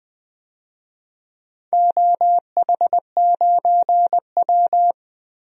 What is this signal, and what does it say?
Morse code, heard as a keyed tone, about 20 wpm, 700 Hz. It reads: OH9W
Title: DXCC-100 Call Signs - 4 Characters - 20wpm